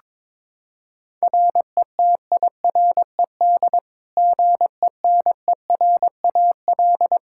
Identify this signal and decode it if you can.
Morse code, heard as a keyed tone, about 21 wpm, 700 Hz. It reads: RETIRED GENERAL
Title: Practice QSO - Exchange 3 - More Info - 22wpm